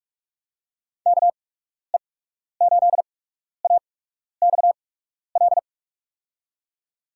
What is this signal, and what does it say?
Morse code, heard as a keyed tone, about 46 wpm, spaced as for 12 wpm, 700 Hz. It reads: KE8AXL
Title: USA Call Signs - 8x Character Spacing - 45wpm